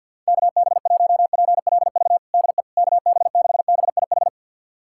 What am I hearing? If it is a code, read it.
Morse code, heard as a keyed tone, about 51 wpm, 700 Hz. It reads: KB1PLV DE KB6BIH
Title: USA Callsign DE Callsign - 6 Characters - 50wpm